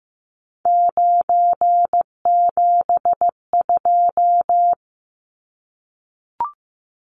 Morse code, heard as a keyed tone, about 15 wpm, 700 Hz. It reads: 972
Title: Numbers - 3-Digit 15wpm